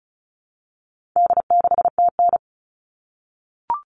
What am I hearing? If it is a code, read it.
Morse code, heard as a keyed tone, about 34 wpm, 700 Hz. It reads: D6TN
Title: All Cut Numbers - 4-Digit 35wpm